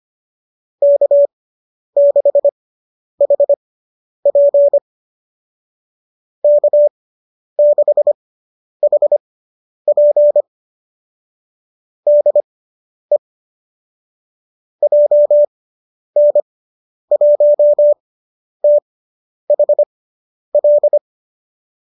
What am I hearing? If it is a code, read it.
Morse code, heard as a keyed tone, about 25 wpm, spaced as for 10 wpm, 590 Hz. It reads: K6HP K6HP DE JN1THL